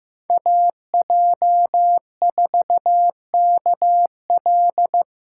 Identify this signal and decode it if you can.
Morse code, heard as a keyed tone, about 15 wpm, 700 Hz. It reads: AJ4KL
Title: USA Call Signs - 5 Characters - 2x Character Spacing - 15wpm